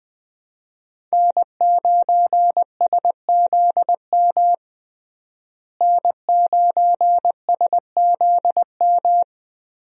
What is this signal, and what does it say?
Morse code, heard as a keyed tone, about 20 wpm, 700 Hz. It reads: N9SZM N9SZM
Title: USA Call Signs - 5 Characters - Repeated 3x - 20wpm